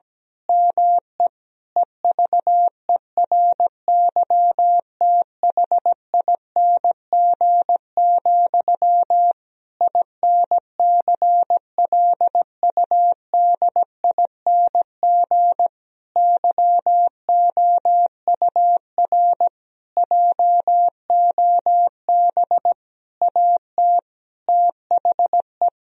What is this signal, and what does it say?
Morse code, heard as a keyed tone, about 17 wpm, 700 Hz. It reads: ME EVERYTHING, INCLUDING YOUR JOB AT THE
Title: Sentences from Top 1000 Words - Review Part 4 - 17wpm